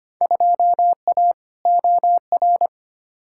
Morse code, heard as a keyed tone, about 25 wpm, 700 Hz. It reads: 2A OR
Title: Field Day Exchanges - 25wpm